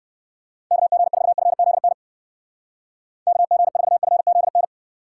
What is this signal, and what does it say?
Morse code, heard as a keyed tone, about 59 wpm, 700 Hz. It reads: KG4FZN KG4FZN